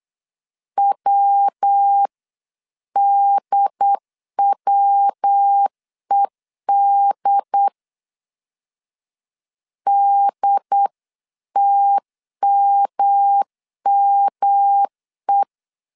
Morse code, heard as a keyed tone, about 8 wpm, 800 Hz. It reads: WDWED DTMME